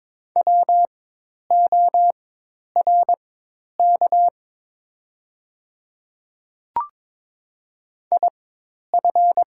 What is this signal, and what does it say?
Morse code, heard as a keyed tone, about 22 wpm, spaced as for 10 wpm, 700 Hz. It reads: WORK IF
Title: 4x Character Spacing - Top 200 Words - 22wpm